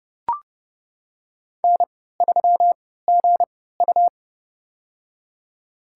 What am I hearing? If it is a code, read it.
Morse code, heard as a keyed tone, about 30 wpm, spaced as for 17 wpm, 700 Hz. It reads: N3GU
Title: USA Call Signs - 4 Characters - 3x Character Spacing - 30wpm